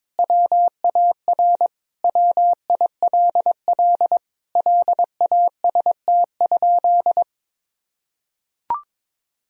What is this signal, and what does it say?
Morse code, heard as a keyed tone, about 22 wpm, 700 Hz. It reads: WAR WILL LAST?
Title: Sentences from Top 1000 Words - Review Part 2 - 22wpm